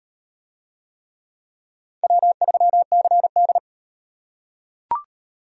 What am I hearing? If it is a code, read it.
Morse code, heard as a keyed tone, about 38 wpm, 700 Hz. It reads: W3CD